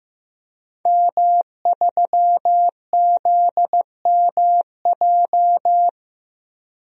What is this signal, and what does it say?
Morse code, heard as a keyed tone, about 15 wpm, 700 Hz. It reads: M3ZMJ
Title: DXCC-100 Call Signs - 5 Characters - Repeated 3x - 15wpm